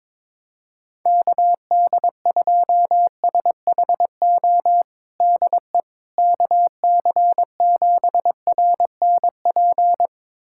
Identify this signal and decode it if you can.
Morse code, heard as a keyed tone, about 22 wpm, 700 Hz. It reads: KD2SHO DE KC7RNP
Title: USA Callsign DE Callsign - 6 Characters - 22wpm